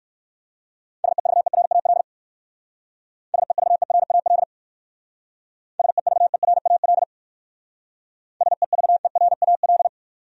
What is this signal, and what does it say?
Morse code, heard as a keyed tone, about 53 wpm, 700 Hz. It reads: SEVERAL SEVERAL SEVERAL SEVERAL
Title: Speed-Racing Top 500 Words 45wpm